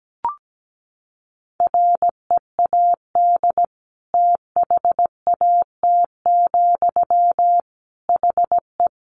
Morse code, heard as a keyed tone, about 17 wpm, 700 Hz. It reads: READ THAT, HE
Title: Sentences from Top 200 Words 17wpm